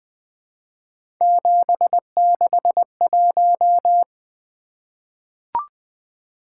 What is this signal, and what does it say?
Morse code, heard as a keyed tone, about 20 wpm, 700 Hz. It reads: 761
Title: Three Letter-Number Combinations - 20wpm